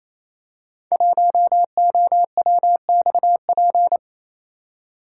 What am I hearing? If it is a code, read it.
Morse code, heard as a keyed tone, about 28 wpm, 700 Hz. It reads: 1OWXP
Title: Code Groups - 5 Letter:Numbers - 28wpm